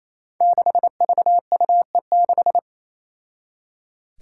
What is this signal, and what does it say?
Morse code, heard as a keyed tone, about 28 wpm, 700 Hz. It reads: 6VUE6
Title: Five Challenging Letter-Number Combinations 28wpm